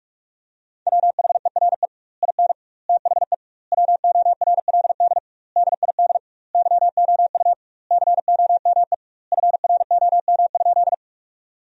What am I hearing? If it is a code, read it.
Morse code, heard as a keyed tone, about 46 wpm, 700 Hz. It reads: WHERE IN THE WORLD DID YOU COME FROM?
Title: 2x Word-Spacing - Sentences from Top 200 Words - 45wpm